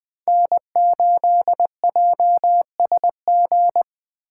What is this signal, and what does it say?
Morse code, heard as a keyed tone, about 20 wpm, 700 Hz. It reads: N8JSG
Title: USA Call Signs - 5 Characters - 4x Character Spacing - 20wpm